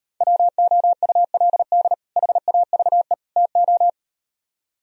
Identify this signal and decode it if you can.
Morse code, heard as a keyed tone, about 38 wpm, 700 Hz. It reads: WOULD HAVE TO